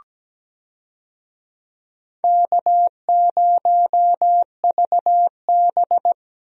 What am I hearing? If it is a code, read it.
Morse code, heard as a keyed tone, about 17 wpm, 700 Hz. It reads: K0VB
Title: USA Call Signs - 4 Characters - 17wpm